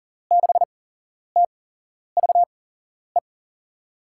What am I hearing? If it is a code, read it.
Morse code, heard as a keyed tone, about 40 wpm, spaced as for 11 wpm, 700 Hz. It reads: 6TVE
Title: USA Call Signs - 8x Character Spacing - 40wpm